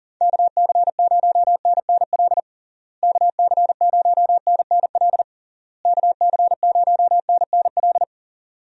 Morse code, heard as a keyed tone, about 40 wpm, 700 Hz. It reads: KC0NNL KC0NNL KC0NNL
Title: USA Call Signs - 6 Characters - Repeated 3x - 40wpm